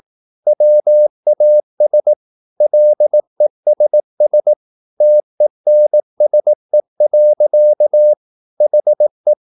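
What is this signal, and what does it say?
Morse code, heard as a keyed tone, about 18 wpm, 600 Hz. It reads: WAS LESS TENSE. HE
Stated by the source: Morse practice at 18 wpm, English text, from CW med Gnister.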